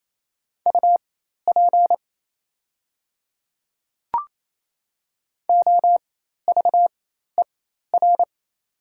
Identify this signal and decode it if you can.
Morse code, heard as a keyed tone, about 28 wpm, spaced as for 13 wpm, 700 Hz. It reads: UP OVER
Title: 4x Character Spacing - Top 100 Words - 28wpm